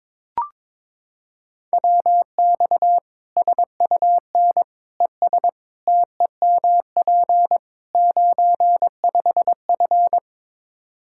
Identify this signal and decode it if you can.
Morse code, heard as a keyed tone, about 22 wpm, 700 Hz. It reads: WX SUN ES TEMP 95F
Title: K1USN SST QSO Practice 22wpm